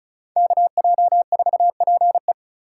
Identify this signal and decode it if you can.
Morse code, heard as a keyed tone, about 34 wpm, 700 Hz. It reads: KJ4PE
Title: USA Call Signs - 5 Characters - 4x Character Spacing - 35wpm